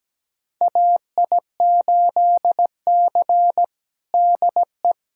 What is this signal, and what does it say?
Morse code, heard as a keyed tone, about 17 wpm, 700 Hz. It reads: AI8C DE
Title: USA Callsign DE Callsign - 4 Characters - 17wpm